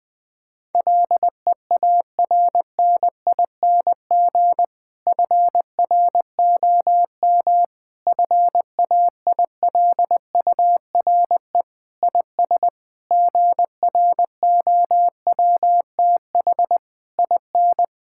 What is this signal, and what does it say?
Morse code, heard as a keyed tone, about 21 wpm, 700 Hz. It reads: LEARNING FROM FAILURE IS GROWTH IN
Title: Difficult Sentences with Hints - 20wpm